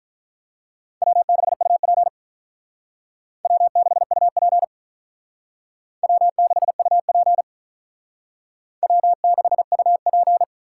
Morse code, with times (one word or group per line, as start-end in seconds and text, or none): 1.0-2.1 W6UP
3.5-4.6 W6UP
6.0-7.4 W6UP
8.8-10.4 W6UP